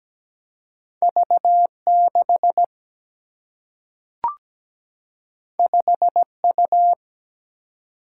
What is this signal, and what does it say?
Morse code, heard as a keyed tone, about 17 wpm, 700 Hz. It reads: V6 5U
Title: Single-Letter Single-Number 17wpm